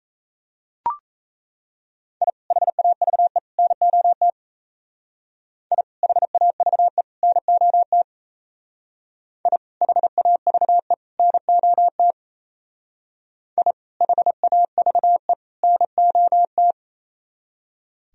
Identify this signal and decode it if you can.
Morse code, heard as a keyed tone, about 34 wpm, 700 Hz. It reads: I HAVE NOT I HAVE NOT I HAVE NOT I HAVE NOT
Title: Speed-Racing - Sets of 3 Words 28wpm